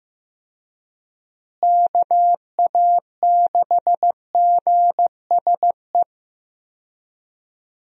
Morse code, heard as a keyed tone, about 15 wpm, 700 Hz. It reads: KA6GSE
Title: USA Call Signs - 6 Characters - Repeated 3x - 15wpm